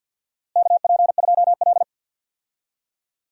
Morse code, heard as a keyed tone, about 51 wpm, 700 Hz. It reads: KP2L
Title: DXCC-100 Call Signs - 4 Characters - Repeated 2x - 50wpm